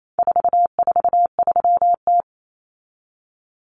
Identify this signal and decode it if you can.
Morse code, heard as a keyed tone, about 28 wpm, 700 Hz. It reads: 443T
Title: Common Cut Numbers - 4-Digit 28wpm